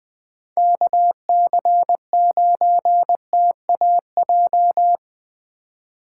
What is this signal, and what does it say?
Morse code, heard as a keyed tone, about 20 wpm, 700 Hz. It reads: KC9TAJ
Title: USA Call Signs - 6 Characters - 4x Character Spacing - 20wpm